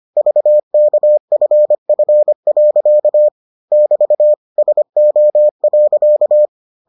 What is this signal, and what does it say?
Morse code, heard as a keyed tone, about 25 wpm, 600 Hz. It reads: VKFF. <BT> SO.